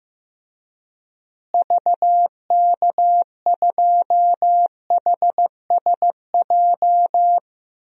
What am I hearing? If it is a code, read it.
Morse code, heard as a keyed tone, about 15 wpm, 700 Hz. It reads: VK2HSJ
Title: DXCC-100 Call Signs - 6 Characters - Repeated 3x - 15wpm